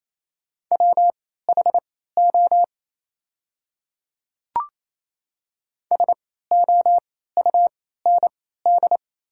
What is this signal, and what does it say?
Morse code, heard as a keyed tone, about 28 wpm, spaced as for 16 wpm, 690 Hz. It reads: WHO SOUND
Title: 3x Character Spacing - Top 300 Words - 28wpm